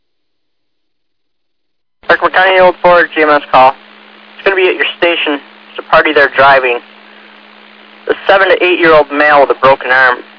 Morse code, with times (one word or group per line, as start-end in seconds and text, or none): none